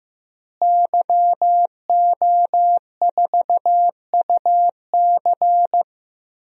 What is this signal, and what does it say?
Morse code, heard as a keyed tone, about 15 wpm, 700 Hz. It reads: YO4UC
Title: DXCC-100 Call Signs - 5 Characters - Repeated 2x - 15wpm